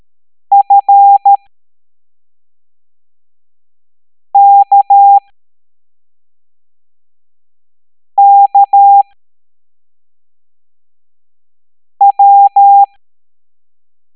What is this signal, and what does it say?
Morse code, heard as a keyed tone, about 13 wpm, spaced as for 3 wpm, 800 Hz. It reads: FKKW